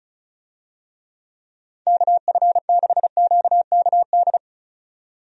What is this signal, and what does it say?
Morse code, heard as a keyed tone, about 34 wpm, 700 Hz. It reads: KF6QKD
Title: USA Call Signs - 6 Characters - 35wpm